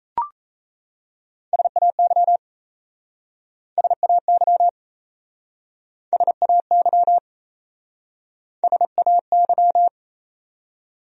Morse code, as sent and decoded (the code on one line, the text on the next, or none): ... .- -.-- / ... .- -.-- / ... .- -.-- / ... .- -.--
SAY SAY SAY SAY